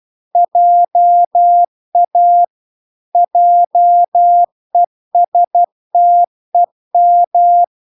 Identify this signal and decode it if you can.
Morse code, heard as a keyed tone, about 12 wpm, 700 Hz. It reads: JA JESTEM